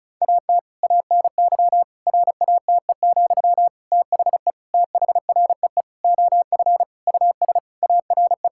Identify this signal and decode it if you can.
Morse code, heard as a keyed tone, about 36 wpm, 700 Hz. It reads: AT ANY RATE, THE THREE OF US ARE
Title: Sentences from Top 1000 Words - Review Part 6 - 35wpm